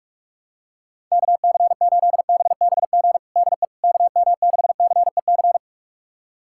Morse code, heard as a keyed tone, about 41 wpm, 700 Hz. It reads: KC8BBG DE KM6CEC
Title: USA Callsign DE Callsign 45wpm